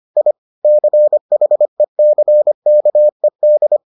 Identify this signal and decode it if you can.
Morse code, heard as a keyed tone, about 25 wpm, 600 Hz. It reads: I CHECKED